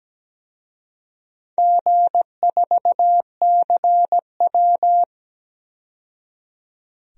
Morse code, heard as a keyed tone, about 17 wpm, 700 Hz. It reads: G4CW
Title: DXCC-100 Call Signs - 4 Characters - 17wpm